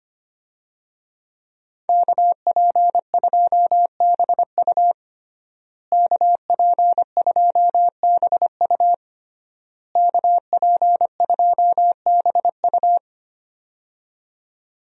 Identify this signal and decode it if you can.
Morse code, heard as a keyed tone, about 25 wpm, 700 Hz. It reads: KP2BU KP2BU KP2BU
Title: USA Call Signs - 5 Characters - Repeated 3x - 25wpm